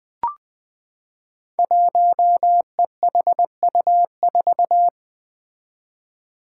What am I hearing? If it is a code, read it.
Morse code, heard as a keyed tone, about 20 wpm, 700 Hz. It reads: 1EHU4
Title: Five Challenging Letter-Number Combinations 20wpm